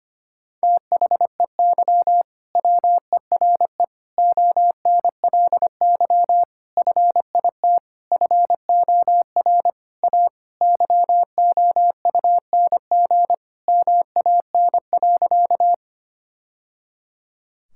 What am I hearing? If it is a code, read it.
Morse code, heard as a keyed tone, about 25 wpm, 700 Hz. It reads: THEY WERE ONLY FIT FOR A YOUNG MAN.
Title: Sentences from Top 1000 Words - Review Part 5 - 25wpm